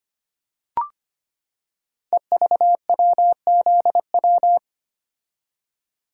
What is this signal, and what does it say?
Morse code, heard as a keyed tone, about 25 wpm, 690 Hz. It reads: EVWZW
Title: Code Groups - 5 Letter:Numbers - 25wpm